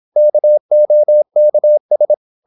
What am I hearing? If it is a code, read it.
Morse code, heard as a keyed tone, about 26 wpm, 600 Hz. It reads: KOKS
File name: Korte ord DK 26wpm